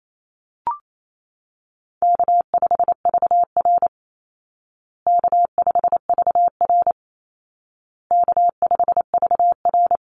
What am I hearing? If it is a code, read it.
Morse code, heard as a keyed tone, about 28 wpm, 700 Hz. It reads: K5VR K5VR K5VR
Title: Easy Call Signs Repeated 3x - 28wpm